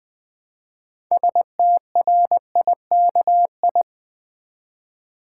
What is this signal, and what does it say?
Morse code, heard as a keyed tone, about 20 wpm, 700 Hz. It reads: STRIKI
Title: Build A Word - 3 Letters At A Time - 8-Letter Words - 20wpm